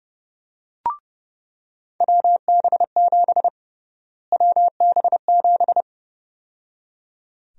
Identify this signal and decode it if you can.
Morse code, heard as a keyed tone, about 32 wpm, 700 Hz. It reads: WB7 WB7
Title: USA Call Sign Prefixes - Repeated 2x - 30wpm